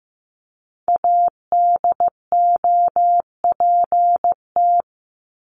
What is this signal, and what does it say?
Morse code, heard as a keyed tone, about 15 wpm, 700 Hz. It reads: ADOPT
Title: ED Suffix - Top 100 Words with Base Words 15wpm